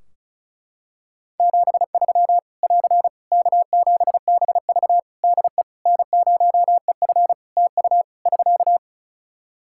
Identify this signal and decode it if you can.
Morse code, heard as a keyed tone, about 34 wpm, 700 Hz. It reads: 73 <AR> K7BV DE N0EF TU <SK>
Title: Practice QSO - Exchange 4 - Ending - 35wpm